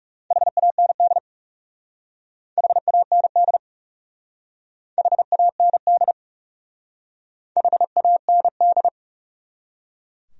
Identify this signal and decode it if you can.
Morse code, heard as a keyed tone, about 37 wpm, 700 Hz. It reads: HAND HAND HAND HAND
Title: Speed-Racing Top 500 Words 30wpm